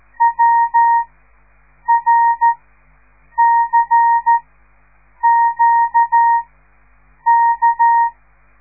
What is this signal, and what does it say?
Morse code, heard as a keyed tone, about 13 wpm, spaced as for 7 wpm, 930 Hz. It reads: WRCQK
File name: radiomorningstar / morse / les 5